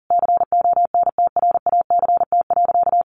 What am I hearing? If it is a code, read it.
Morse code, heard as a keyed tone, about 40 wpm, 700 Hz. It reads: CONTRACT.
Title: Sentences from Top 1400 Words 40wpm